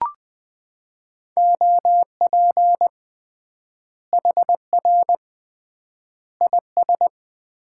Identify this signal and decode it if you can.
Morse code, heard as a keyed tone, about 20 wpm, 700 Hz. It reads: OP HR IS
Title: Common QSO Phrases - 3x Word-Spacing - 20wpm